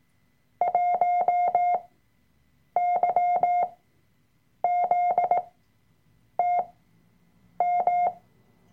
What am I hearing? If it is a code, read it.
Morse code, heard as a keyed tone, about 18 wpm, spaced as for 7 wpm, 710 Hz. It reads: 1YZTM
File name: Workshop Dateien / CW-Online-Kurs / Lektion 003